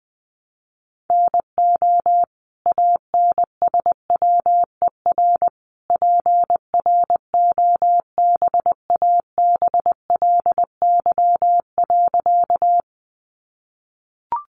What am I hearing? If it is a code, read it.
Morse code, heard as a keyed tone, about 20 wpm, 700 Hz. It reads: NO ANSWER PROBABLY.
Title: Sentences from Top 700 Words 20wpm